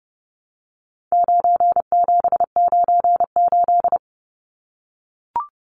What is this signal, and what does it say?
Morse code, heard as a keyed tone, about 28 wpm, 700 Hz. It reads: 9798
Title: Numbers - 4-Digit 30wpm